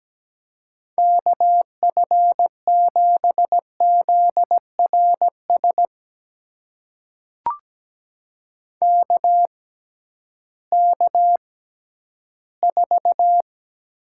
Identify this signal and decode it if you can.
Morse code, heard as a keyed tone, about 17 wpm, spaced as for 11 wpm, 700 Hz. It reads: KF7ZRS KK4
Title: USA Call Signs - 6 Characters - 6x Character Spacing - 17wpm